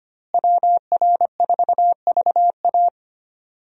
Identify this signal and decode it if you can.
Morse code, heard as a keyed tone, about 25 wpm, 700 Hz. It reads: WR4VA
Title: USA Call Signs - 5 Characters - Repeated 2x - 25wpm